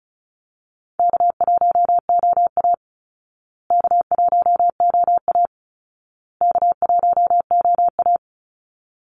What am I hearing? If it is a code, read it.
Morse code, heard as a keyed tone, about 34 wpm, 700 Hz. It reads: K1OA K1OA K1OA
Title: Call Signs Repeated 2x - 35wpm